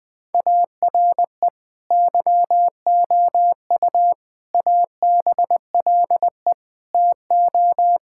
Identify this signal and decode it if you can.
Morse code, heard as a keyed tone, about 20 wpm, 700 Hz. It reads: ARE YOU ABLE TO
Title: Sentences from Top 1000 Words - Review Part 5 - 20wpm